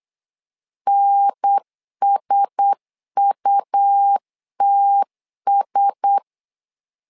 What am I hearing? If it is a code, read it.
Morse code, heard as a keyed tone, about 8 wpm, 800 Hz. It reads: NSUTS
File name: LEZIONI / 4 Q1S 8wpm 2710